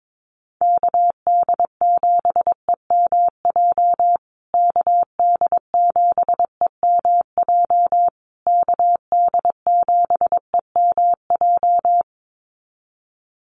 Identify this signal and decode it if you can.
Morse code, heard as a keyed tone, about 22 wpm, 700 Hz. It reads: KD7EMJ KD7EMJ KD7EMJ
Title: USA Call Signs Repeated 3x 22wpm